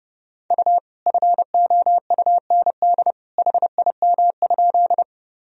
Morse code, heard as a keyed tone, about 29 wpm, 700 Hz. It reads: U FOUND HIM?
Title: Sentences from Top 300 Words and Abbreviations 30wpm